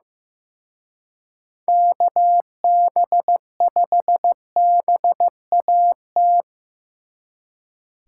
Morse code, heard as a keyed tone, about 17 wpm, 700 Hz. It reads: KB5BAT
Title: Speed Racing - USA Call Signs 15wpm